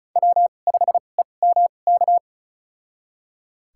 Morse code, heard as a keyed tone, about 36 wpm, spaced as for 25 wpm, 700 Hz. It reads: W5EMK
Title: USA Call Signs - 5 Characters - 2x Character Spacing - 35wpm